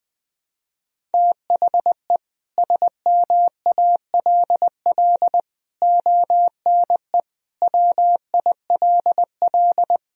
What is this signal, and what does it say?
Morse code, heard as a keyed tone, about 20 wpm, 700 Hz. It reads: THE SMALL ONE WILL
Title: Sentences from Top 1000 Words - Review Part 1 - 20wpm